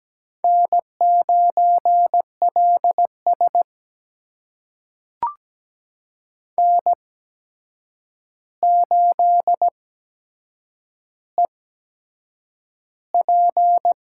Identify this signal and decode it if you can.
Morse code, heard as a keyed tone, about 17 wpm, spaced as for 7 wpm, 700 Hz. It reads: N9LS N8EP
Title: USA Call Signs - 8x Character Spacing - 17wpm